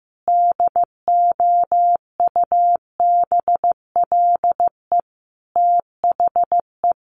Morse code, heard as a keyed tone, about 15 wpm, 700 Hz. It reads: DOUBLE THE
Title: Sentences from Top 1400 Words 15wpm